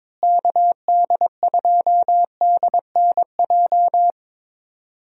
Morse code, heard as a keyed tone, about 22 wpm, 700 Hz. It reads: KD2DNJ
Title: USA Call Signs - 6 Characters - Repeated 2x - 22wpm